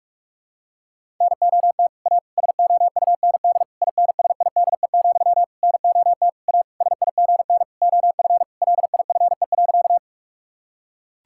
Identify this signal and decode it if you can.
Morse code, heard as a keyed tone, about 46 wpm, 700 Hz. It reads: NOT A SOUND INSIDE, NOT A SIGN OF LIFE.
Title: Sentences from Top 1000 Words - Review Part 6 - 45wpm